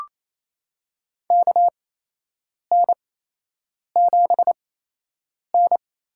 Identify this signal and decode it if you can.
Morse code, heard as a keyed tone, about 28 wpm, spaced as for 8 wpm, 700 Hz. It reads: KN7N
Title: USA Call Signs - 8x Character Spacing - 28wpm